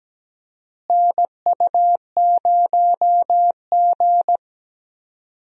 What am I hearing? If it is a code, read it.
Morse code, heard as a keyed tone, about 17 wpm, 700 Hz. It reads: NU0G
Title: USA Call Signs - 4 Characters - Repeated 2x - 17wpm